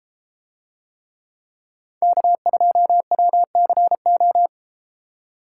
Morse code, heard as a keyed tone, about 30 wpm, 700 Hz. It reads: K2WCO